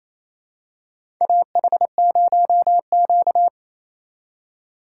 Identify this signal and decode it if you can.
Morse code, heard as a keyed tone, about 28 wpm, 690 Hz. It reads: AH0Q
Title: USA Call Signs - 4 Characters - 2x Character Spacing - 28wpm